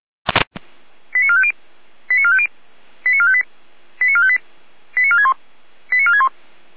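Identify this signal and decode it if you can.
Five-tone selective call (ZVEI), heard as quick runs of five tones, tones of about 70 ms. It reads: 89450 89450 89458 89458 89461 89461
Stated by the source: Five-tone ZVEI call 89127 received by SDR, recorded by Feuerwehr Orlamünde.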